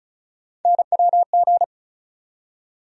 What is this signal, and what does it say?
Morse code, heard as a keyed tone, about 36 wpm, 700 Hz. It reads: NWG
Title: USA Call Sign Postfixes - 3 Letters 35wpm